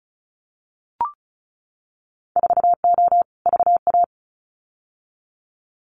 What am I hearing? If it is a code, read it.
Morse code, heard as a keyed tone, about 36 wpm, 700 Hz. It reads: 4O VA
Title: Winter Field Day Exchanges Encore 35wpm